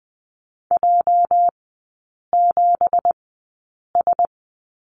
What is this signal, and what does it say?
Morse code, heard as a keyed tone, about 20 wpm, spaced as for 8 wpm, 700 Hz. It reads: J7S
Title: 205 Lesson 33e - Letter j - Three Characters 20wpm